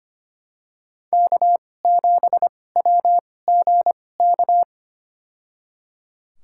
Morse code, heard as a keyed tone, about 25 wpm, spaced as for 18 wpm, 700 Hz. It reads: K7WGK
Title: USA Call Signs - 5 Characters - 2x Character Spacing - 25wpm